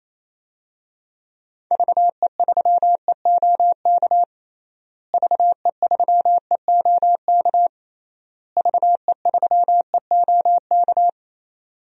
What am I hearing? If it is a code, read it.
Morse code, heard as a keyed tone, about 28 wpm, 700 Hz. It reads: VE3EOK VE3EOK VE3EOK
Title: DXCC-100 Call Signs - 6 Characters - Repeated 3x - 28wpm